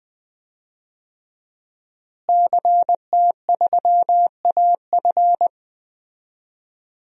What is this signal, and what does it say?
Morse code, heard as a keyed tone, about 20 wpm, 700 Hz. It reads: CT3AF
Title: DXCC-100 Call Signs - 5 Characters - Repeated 2x - 20wpm